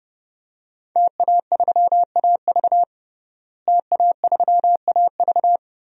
Morse code, heard as a keyed tone, about 30 wpm, 700 Hz. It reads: TA3AV TA3AV
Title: DXCC-100 Call Signs - 5 Characters - Repeated 2x - 30wpm